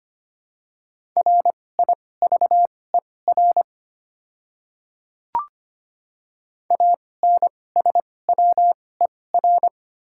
Morse code, heard as a keyed tone, about 25 wpm, spaced as for 18 wpm, 690 Hz. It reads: RIVER ANSWER
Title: 2x Character Spacing - Top 300 Words - 25wpm